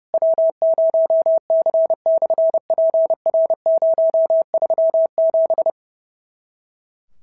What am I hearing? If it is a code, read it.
Morse code, heard as a keyed tone, about 30 wpm, 650 Hz. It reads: W0C/PR037
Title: SOTA QSO Practice 30wpm